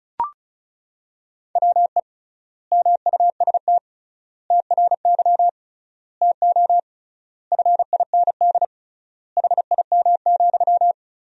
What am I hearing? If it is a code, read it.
Morse code, heard as a keyed tone, about 36 wpm, 700 Hz. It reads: WE MUST TRY TO FIND HIM,
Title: 3x Word-Spacing - Sentences from Top 400 Words - 35wpm